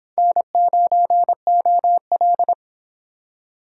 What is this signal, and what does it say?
Morse code, heard as a keyed tone, about 26 wpm, 700 Hz. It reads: N9OL